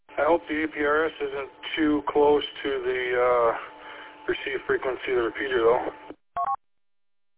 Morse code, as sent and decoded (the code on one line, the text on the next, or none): none